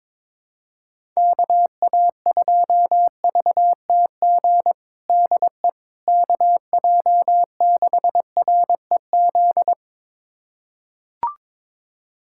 Morse code, heard as a keyed tone, about 22 wpm, 700 Hz. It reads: KA2VTG DE KJ6REZ
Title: USA Callsign DE Callsign - 6 Characters - 22wpm